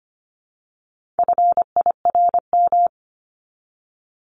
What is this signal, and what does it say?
Morse code, heard as a keyed tone, about 25 wpm, 700 Hz. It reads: FIRM
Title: Top 1251-1500 Words 25wpm